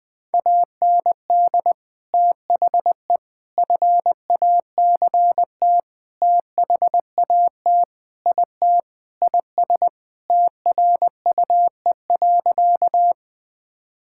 Morse code, heard as a keyed tone, about 21 wpm, 700 Hz. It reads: AND THE FACT THAT IT IS TRUE.
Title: Sentences with 4-Word Phrases - 20wpm